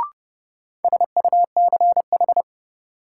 Morse code, heard as a keyed tone, about 30 wpm, 700 Hz. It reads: SUCH
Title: Sentences from Top 200 Words - New Word Drill - 30wpm